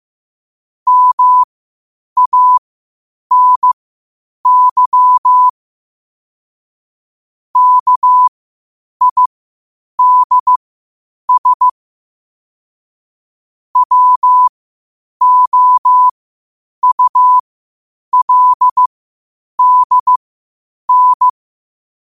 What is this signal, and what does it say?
Morse code, heard as a keyed tone, about 15 wpm, spaced as for 8 wpm, 1000 Hz. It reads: MANY KIDS WOULDN
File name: qotdfiles / 8wpm